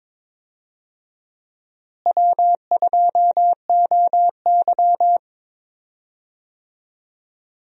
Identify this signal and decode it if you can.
Morse code, heard as a keyed tone, about 22 wpm, 700 Hz. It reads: W2OY
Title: USA Call Signs - 4 Characters - Repeated 3x - 22wpm